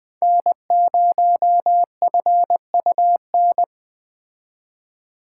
Morse code, heard as a keyed tone, about 20 wpm, 700 Hz. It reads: N0FUN